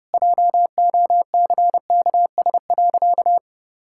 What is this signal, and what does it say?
Morse code, heard as a keyed tone, about 30 wpm, 700 Hz. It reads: JOCKS.
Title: Pangram Challenge v2 - 30wpm